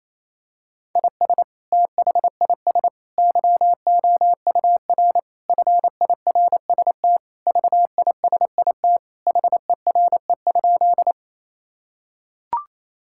Morse code, heard as a keyed tone, about 28 wpm, 700 Hz. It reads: IS THIS YOUR FIRST VISIT HERE?
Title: Sentences from Top 1000 Words - Review Part 6 - 28wpm